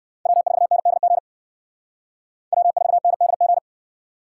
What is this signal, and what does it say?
Morse code, heard as a keyed tone, about 57 wpm, 700 Hz. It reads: W4NDG W4NDG